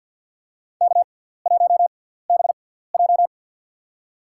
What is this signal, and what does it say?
Morse code, heard as a keyed tone, about 53 wpm, spaced as for 17 wpm, 690 Hz. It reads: K1BJ
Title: USA Call Signs - 4 Characters - 6x Character Spacing - 50wpm